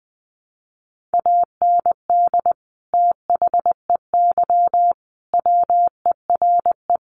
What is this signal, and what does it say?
Morse code, heard as a keyed tone, about 20 wpm, 700 Hz. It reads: AND THEY WERE
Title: Sentences from Top 1300 Words 20wpm